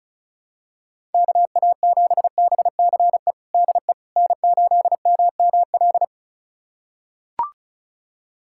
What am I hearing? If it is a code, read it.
Morse code, heard as a keyed tone, about 36 wpm, 700 Hz. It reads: KA7BCE DE N8MML
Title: USA Callsign DE Callsign 35wpm